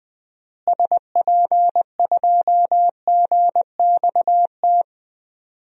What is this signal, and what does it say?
Morse code, heard as a keyed tone, about 21 wpm, 700 Hz. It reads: SP2GXT
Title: DXCC-100 Call Signs - 6 Characters - Repeated 2x - 20wpm